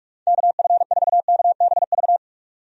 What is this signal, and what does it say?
Morse code, heard as a keyed tone, about 46 wpm, 700 Hz. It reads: KF4KBV
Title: USA Call Signs - 6 Characters - 45wpm